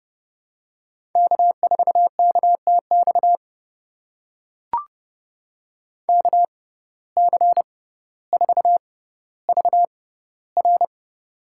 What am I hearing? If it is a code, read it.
Morse code, heard as a keyed tone, about 30 wpm, spaced as for 15 wpm, 700 Hz. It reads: K4KTX KC4VR
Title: USA Call Signs - 5 Characters - 6x Character Spacing - 30wpm